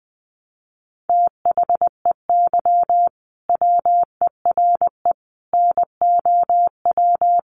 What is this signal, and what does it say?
Morse code, heard as a keyed tone, about 19 wpm, 700 Hz. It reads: THEY WERE NOW
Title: Sentences from Top 1400 Words 20wpm